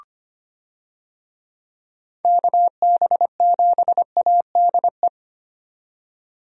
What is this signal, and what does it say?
Morse code, heard as a keyed tone, about 25 wpm, 700 Hz. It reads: KB7ADE
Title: USA Call Signs - 6 Characters - 25wpm